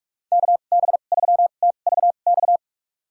Morse code, heard as a keyed tone, about 46 wpm, spaced as for 33 wpm, 700 Hz. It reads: KB3TVX
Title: USA Call Signs - 6 Characters - 2x Character Spacing - 45wpm